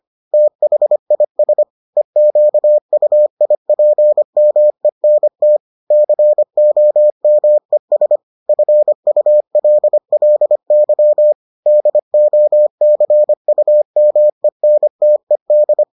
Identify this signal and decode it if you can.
Morse code, heard as a keyed tone, about 25 wpm, 600 Hz. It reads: THIS EQUIPMENT COMES FULLY DOCUMENTED